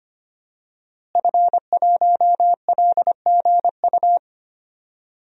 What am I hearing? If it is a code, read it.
Morse code, heard as a keyed tone, about 25 wpm, 700 Hz. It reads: F1LGU
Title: DXCC-100 Call Signs - 5 Characters - Repeated 2x - 25wpm